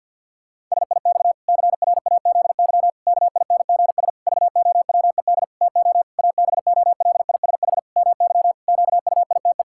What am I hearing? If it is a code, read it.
Morse code, heard as a keyed tone, about 50 wpm, 700 Hz. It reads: SIX CRAZY KINGS VOWED TO ABOLISH MY QUITE
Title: Pangram Challenge v2 - 50wpm